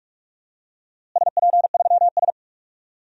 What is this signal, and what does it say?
Morse code, heard as a keyed tone, about 42 wpm, 700 Hz. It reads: SP3S
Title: DXCC-100 Call Signs - 4 Characters - Repeated 2x - 45wpm